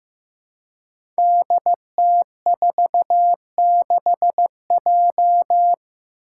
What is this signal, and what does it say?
Morse code, heard as a keyed tone, about 15 wpm, 700 Hz. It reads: DT46J
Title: Five Challenging Letter-Number Combinations 15wpm